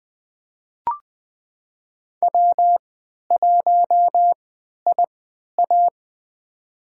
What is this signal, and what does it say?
Morse code, heard as a keyed tone, about 20 wpm, spaced as for 11 wpm, 700 Hz. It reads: W1IA
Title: USA Call Signs - 4 Characters - 3x Character Spacing - 20wpm